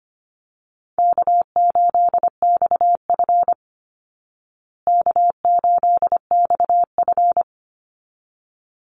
Morse code, heard as a keyed tone, about 25 wpm, 700 Hz. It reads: K8XF K8XF
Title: Easy Call Signs Repeated 2x - 25wpm